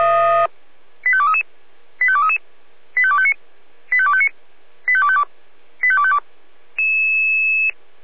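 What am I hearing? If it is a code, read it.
Five-tone selective call (ZVEI), heard as quick runs of five tones, tones of about 70 ms. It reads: 85233 85233 85259 85259 85252 85252
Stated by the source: ZVEI five-tone selective call received by SDR.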